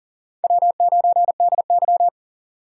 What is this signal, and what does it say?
Morse code, heard as a keyed tone, about 40 wpm, 700 Hz. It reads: W9DY
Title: K1USN SST QSO Practice - 40wpm